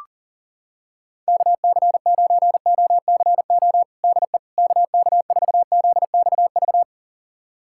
Morse code, heard as a keyed tone, about 40 wpm, 700 Hz. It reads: KC9OCO DE KK4ZXV
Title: USA Callsign DE Callsign 40wpm